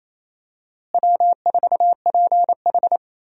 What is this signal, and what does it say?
Morse code, heard as a keyed tone, about 28 wpm, 700 Hz. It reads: W4PH
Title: USA Call Signs - 4 Characters - Repeated 2x - 28wpm